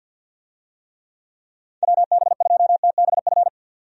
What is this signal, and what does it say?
Morse code, heard as a keyed tone, about 46 wpm, 700 Hz. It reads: WB2TBF
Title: USA Call Signs - 6 Characters - 50wpm